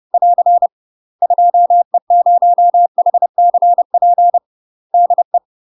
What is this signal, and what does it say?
Morse code, heard as a keyed tone, about 30 wpm, 700 Hz. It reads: <AR> 2E0HCP DE
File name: UK MASTER QSOs TRAINING / 30WPM / UK MASTER QSOs 0010